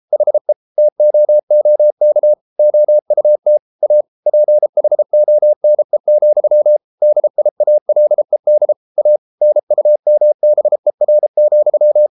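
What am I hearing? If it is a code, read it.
Morse code, heard as a keyed tone, about 33 wpm, 600 Hz. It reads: HE TOOK OUT A PHONE, DIALED A NUMBER,
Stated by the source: Morse code practice at 32 wpm from CW med Gnister, reading Ken Follett's novel Never in English.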